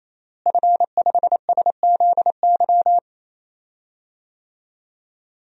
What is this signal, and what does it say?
Morse code, heard as a keyed tone, about 27 wpm, 700 Hz. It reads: F5SZY
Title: DXCC-100 Call Signs - 5 Characters - Repeated 3x - 28wpm